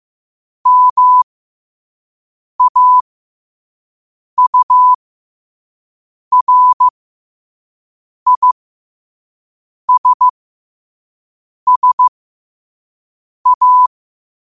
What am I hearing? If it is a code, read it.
Morse code, heard as a keyed tone, about 15 wpm, spaced as for 5 wpm, 1000 Hz. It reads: MAURISSA
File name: qotdfiles / 5wpm